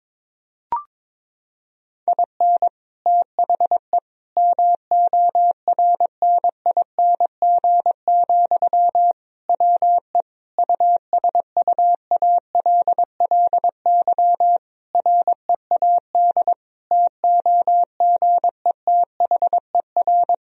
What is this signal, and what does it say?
Morse code, heard as a keyed tone, about 22 wpm, 700 Hz. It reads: IN THE MORNING, WE USUALLY READ TOGETHER
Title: Sentences from Top 1000 Words - Review Part 3 - 22wpm